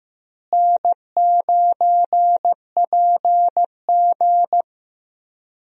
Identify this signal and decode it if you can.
Morse code, heard as a keyed tone, about 15 wpm, 700 Hz. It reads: N9PG
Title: USA Call Signs - 4 Characters - Repeated 2x - 15wpm